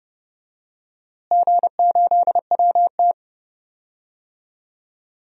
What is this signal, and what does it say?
Morse code, heard as a keyed tone, about 30 wpm, 700 Hz. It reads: G8WT
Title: DXCC-100 Call Signs - 4 Characters - Repeated 2x - 30wpm